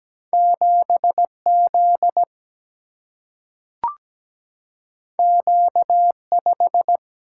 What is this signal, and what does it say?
Morse code, heard as a keyed tone, about 17 wpm, 700 Hz. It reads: 7Z Q5
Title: Single-Letter Single-Number 17wpm